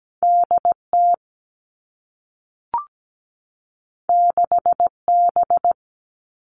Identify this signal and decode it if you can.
Morse code, heard as a keyed tone, about 17 wpm, 700 Hz. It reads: DT 6B
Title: All Cut Numbers - 2-Digit 17wpm